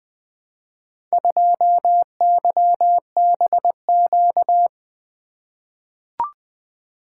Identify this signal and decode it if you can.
Morse code, heard as a keyed tone, about 20 wpm, 700 Hz. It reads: 2YBQ
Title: Four Letter-Number Combinations - 20wpm